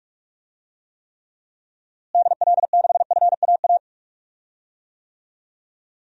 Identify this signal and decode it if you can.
Morse code, heard as a keyed tone, about 46 wpm, 700 Hz. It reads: DL6FAA
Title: DXCC-100 Call Signs - 6 Characters - Repeated 3x - 45wpm